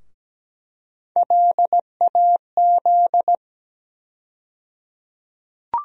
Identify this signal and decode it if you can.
Morse code, heard as a keyed tone, about 17 wpm, 700 Hz. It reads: LAZ
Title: USA Call Sign Postfixes - 3 Letters 17wpm